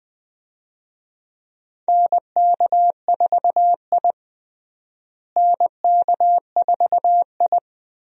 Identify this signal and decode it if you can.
Morse code, heard as a keyed tone, about 20 wpm, 700 Hz. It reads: NK4I NK4I
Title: USA Call Signs - 4 Characters - Repeated 2x - 20wpm